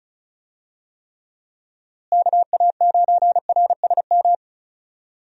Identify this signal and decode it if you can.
Morse code, heard as a keyed tone, about 34 wpm, 700 Hz. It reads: KA9RSM
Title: USA Call Signs - 6 Characters - Repeated 3x - 35wpm